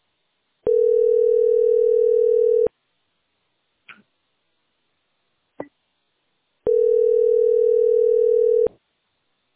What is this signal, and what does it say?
Ringback tone (North America), 440 + 480 Hz.